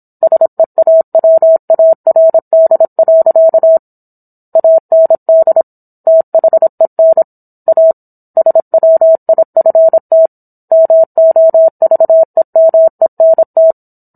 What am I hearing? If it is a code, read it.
Morse code, heard as a keyed tone, about 26 wpm, 650 Hz. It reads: SEAWARD. AND THEN A SWIFT MOVEMENT